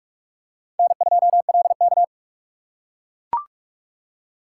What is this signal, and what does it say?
Morse code, heard as a keyed tone, about 44 wpm, 700 Hz. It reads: N2LK
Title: K1USN SST QSO Practice - 45wpm